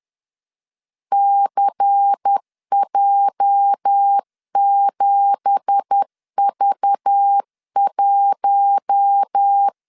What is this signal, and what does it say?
Morse code, heard as a keyed tone, about 10 wpm, 800 Hz. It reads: CJ7V1
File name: LEZIONI / 9 2867 10wpm 2733